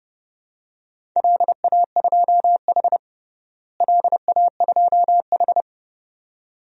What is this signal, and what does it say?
Morse code, heard as a keyed tone, about 32 wpm, 700 Hz. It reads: LA2H LA2H
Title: DXCC-100 Call Signs - 4 Characters - Repeated 2x - 30wpm